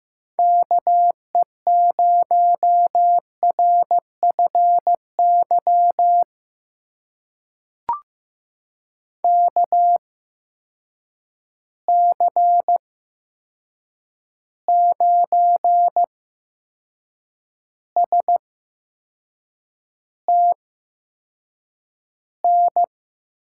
Morse code, heard as a keyed tone, about 15 wpm, spaced as for 6 wpm, 700 Hz. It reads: KE0RFY KC9STN
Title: USA Call Signs - 6 Characters - 8x Character Spacing - 15wpm